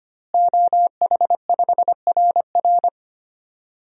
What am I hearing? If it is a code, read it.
Morse code, heard as a keyed tone, about 25 wpm, 700 Hz. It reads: OH5RR
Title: DXCC-100 Call Signs - 5 Characters - Repeated 3x - 25wpm